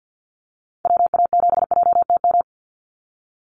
Morse code, heard as a keyed tone, about 50 wpm, 700 Hz. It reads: WA7JTM
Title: NA SOTA Chaser Callsigns 50wpm